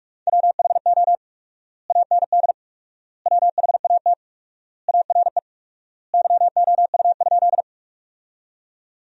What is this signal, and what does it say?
Morse code, heard as a keyed tone, about 42 wpm, 700 Hz. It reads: WHO AND WHAT ARE YOU?
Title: 4x Word-Spacing - Sentences from Top 100 Words - 45wpm